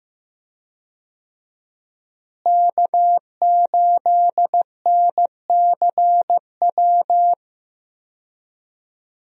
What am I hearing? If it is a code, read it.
Morse code, heard as a keyed tone, about 15 wpm, 700 Hz. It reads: K8NCW
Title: USA Call Signs - 5 Characters - Repeated 2x - 15wpm